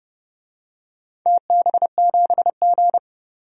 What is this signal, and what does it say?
Morse code, heard as a keyed tone, about 30 wpm, 700 Hz. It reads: TB7G
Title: DXCC-100 Call Signs - 4 Characters - 30wpm